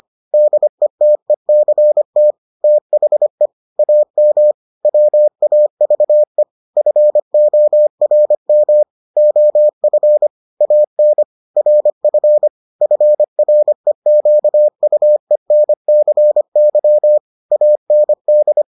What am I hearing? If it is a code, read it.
Morse code, heard as a keyed tone, about 25 wpm, 600 Hz. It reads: DETECT THE AM WAVE FORM OF AN RF FREQUENCY AND